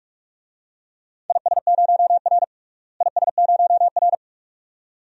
Morse code, heard as a keyed tone, about 44 wpm, 700 Hz. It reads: IS0R IS0R
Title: DXCC-100 Call Signs - 4 Characters - Repeated 2x - 45wpm